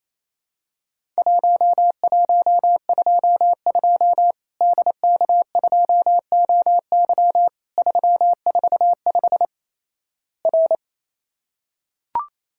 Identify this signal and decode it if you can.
Morse code, heard as a keyed tone, about 28 wpm, 700 Hz. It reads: 1122 DK2OY 345 R
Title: Worked All Europe DX Contest - QTC Exchange - 28wpm